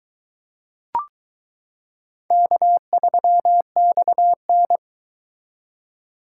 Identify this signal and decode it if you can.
Morse code, heard as a keyed tone, about 22 wpm, 700 Hz. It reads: K3XN